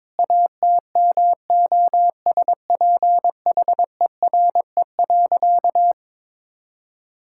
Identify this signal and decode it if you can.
Morse code, heard as a keyed tone, about 22 wpm, 700 Hz. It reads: ATMOSPHERE.
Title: Sentences from Top 1600 Words 22wpm